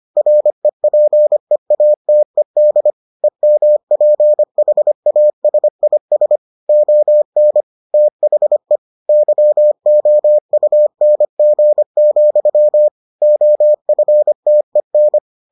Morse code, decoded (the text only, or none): REPEATED EMPHASIS ON THE YOUNG, OFTEN